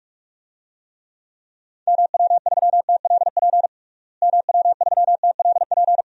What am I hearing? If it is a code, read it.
Morse code, heard as a keyed tone, about 46 wpm, 700 Hz. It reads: MW3TLP MW3TLP
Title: DXCC-100 Call Signs - 6 Characters - Repeated 2x - 45wpm